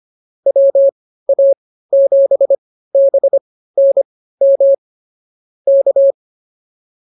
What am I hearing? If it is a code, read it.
Morse code, heard as a keyed tone, about 24 wpm, spaced as for 15 wpm, 550 Hz. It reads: WA7BNM K
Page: intermediate-practice-files